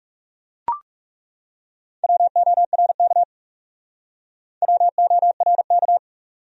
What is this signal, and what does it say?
Morse code, heard as a keyed tone, about 36 wpm, 700 Hz. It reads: WORK WORK
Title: Speed-Racing Top 300 Words 30wpm